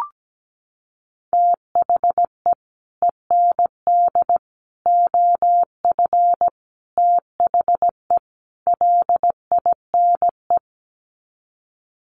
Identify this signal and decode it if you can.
Morse code, heard as a keyed tone, about 17 wpm, 700 Hz. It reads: THE END OF THE LINE
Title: Sets of 5 Words Encore2 17wpm